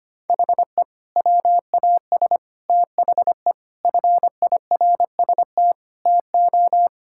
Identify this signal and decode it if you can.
Morse code, heard as a keyed tone, about 25 wpm, 700 Hz. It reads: HE WAS THE FIRST TO
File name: practice / 25wpm / Sentences with 2-Word Phrases - 25wpm